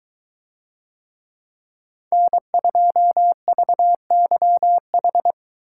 Morse code, heard as a keyed tone, about 23 wpm, 700 Hz. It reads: N2VYH